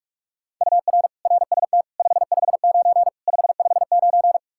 Morse code, heard as a keyed tone, about 46 wpm, 700 Hz. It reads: UR RST 559 559
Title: Practice QSO - Exchange 1 - Beginning - 45wpm